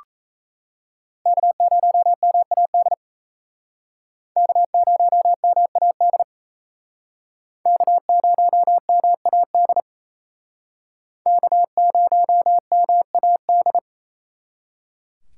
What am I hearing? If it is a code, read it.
Morse code, heard as a keyed tone, about 35 wpm, 700 Hz. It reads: K0MAD K0MAD K0MAD K0MAD
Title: Speed Racing - USA 1x3 Call Signs 28wpm